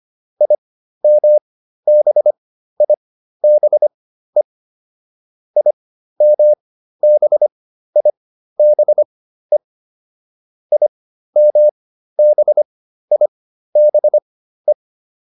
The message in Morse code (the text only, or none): IMBIBE IMBIBE IMBIBE